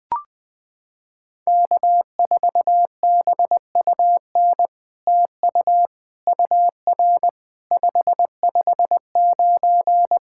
Morse code, heard as a keyed tone, about 20 wpm, 700 Hz. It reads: K4BUN TU UR 559
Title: SOTA QSO Practice 20wpm